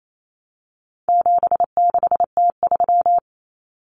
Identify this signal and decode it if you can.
Morse code, heard as a keyed tone, about 28 wpm, 700 Hz. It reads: 76T3
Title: Common Cut Numbers - 4-Digit 28wpm